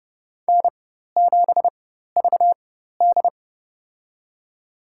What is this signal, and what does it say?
Morse code, heard as a keyed tone, about 30 wpm, spaced as for 14 wpm, 700 Hz. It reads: N7VD
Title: USA Call Signs - 4x Character Spacing - 30wpm